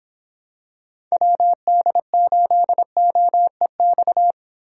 Morse code, heard as a keyed tone, about 26 wpm, 700 Hz. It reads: WD8OEX